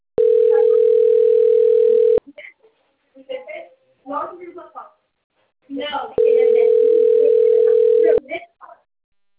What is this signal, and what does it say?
Ringback tone (North America), 440 + 480 Hz, with speech.
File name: rg-1600-unknown- - - .11157